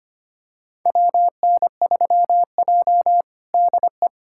Morse code, heard as a keyed tone, about 25 wpm, 700 Hz. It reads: WN3J DE
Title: USA Callsign DE Callsign - 4 Characters - 25wpm